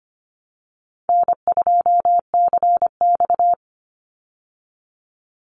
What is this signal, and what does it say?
Morse code, heard as a keyed tone, about 25 wpm, 700 Hz. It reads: N2CX
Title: Call Signs Repeated 2x - 25wpm